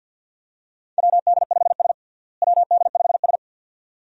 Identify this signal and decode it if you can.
Morse code, heard as a keyed tone, about 48 wpm, 700 Hz. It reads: WD5S WD5S
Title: USA Call Signs - 4 Characters - Repeated 2x - 50wpm